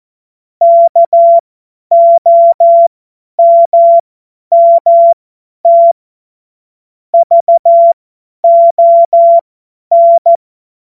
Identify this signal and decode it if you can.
Morse code, heard as a keyed tone, about 14 wpm, spaced as for 10 wpm, 680 Hz. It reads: KOMMT VON
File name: Morsekurs / Witze / Witz 18. Januar 2022, 50 F70